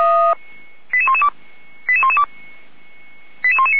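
Five-tone selective call (ZVEI), heard as quick runs of five tones, tones of about 70 ms. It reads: 88112 88112 88119
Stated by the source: ZVEI five-tone selective call received via SDR.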